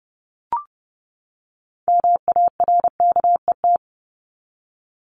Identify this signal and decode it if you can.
Morse code, heard as a keyed tone, about 30 wpm, 700 Hz. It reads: MARKET
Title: ING Suffix - Top 100 Words with Base Words 30wpm